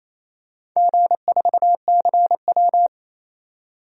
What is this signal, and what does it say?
Morse code, heard as a keyed tone, about 28 wpm, 700 Hz. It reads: G4CW
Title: DXCC-100 Call Signs - 4 Characters - 28wpm